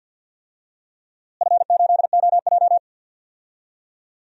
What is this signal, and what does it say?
Morse code, heard as a keyed tone, about 53 wpm, 700 Hz. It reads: F8OJ
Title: DXCC-100 Call Signs - 4 Characters - Repeated 3x - 50wpm